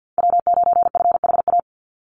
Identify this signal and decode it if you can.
Morse code, heard as a keyed tone, about 49 wpm, 700 Hz. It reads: W9FHA
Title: NA SOTA Chaser Callsigns 50wpm